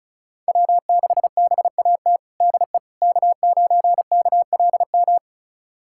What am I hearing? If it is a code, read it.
Morse code, heard as a keyed tone, about 36 wpm, 700 Hz. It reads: W6BAT DE K9KLM
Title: USA Callsign DE Callsign - 5 Characters - 35wpm